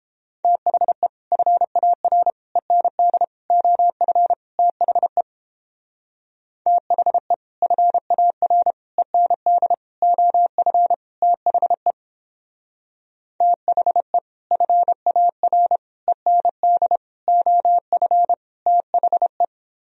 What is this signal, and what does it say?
Morse code, heard as a keyed tone, about 29 wpm, 700 Hz. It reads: THE FAR END OF THE THE FAR END OF THE THE FAR END OF THE